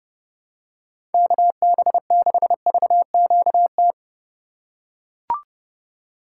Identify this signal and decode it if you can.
Morse code, heard as a keyed tone, about 31 wpm, 700 Hz. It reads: KB6VQT
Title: USA Call Signs - 6 Characters - 4x Character Spacing - 30wpm